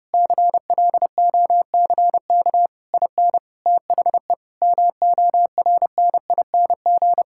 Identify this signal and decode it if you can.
Morse code, heard as a keyed tone, about 28 wpm, 700 Hz. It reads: CLOCK IN THE MORNING
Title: Pangram Challenge v2 - 30wpm